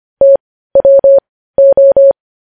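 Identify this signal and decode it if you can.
Morse code, heard as a keyed tone, about 25 wpm, spaced as for 15 wpm, 550 Hz. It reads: TWO
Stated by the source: Morse news headlines sent at 15 words per minute.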